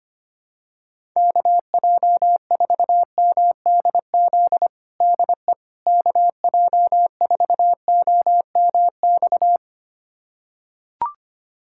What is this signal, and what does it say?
Morse code, heard as a keyed tone, about 25 wpm, 700 Hz. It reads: KJ4MDZ DE KJ4OMX
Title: USA Callsign DE Callsign - 6 Characters - 25wpm